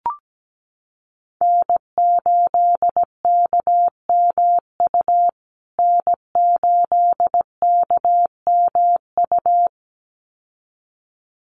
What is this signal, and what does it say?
Morse code, heard as a keyed tone, about 17 wpm, 700 Hz. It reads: N8KMU N8KMU
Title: USA Call Signs Repeated 2x 17wpm